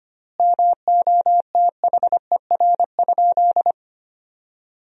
Morse code, heard as a keyed tone, about 25 wpm, 700 Hz. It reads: MOTHER?
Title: 2x Word-Spacing - Sentences from Top 200 Words - 25wpm